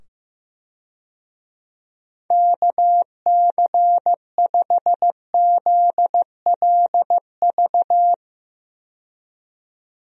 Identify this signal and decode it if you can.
Morse code, heard as a keyed tone, about 15 wpm, 700 Hz. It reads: KC5ZLV
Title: USA Call Signs - 6 Characters - 15wpm